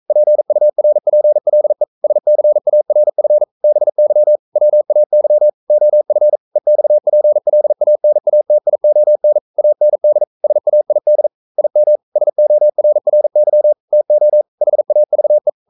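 Morse code, heard as a keyed tone, about 42 wpm, 600 Hz. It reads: PURPLE SCARF BY WAY OF EXPLANATION AND SAID IM SORRY TO HAVE
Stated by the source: Morse code sending of an English novel at 42 wpm.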